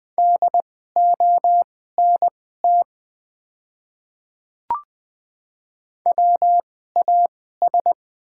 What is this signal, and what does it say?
Morse code, heard as a keyed tone, about 20 wpm, spaced as for 14 wpm, 700 Hz. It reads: DONT WAS
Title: 2x Character Spacing - Top 300 Words - 20wpm